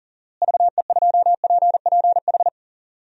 Morse code, heard as a keyed tone, about 40 wpm, 700 Hz. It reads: VE2PPH
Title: DXCC-100 Call Signs - 6 Characters - Repeated 3x - 40wpm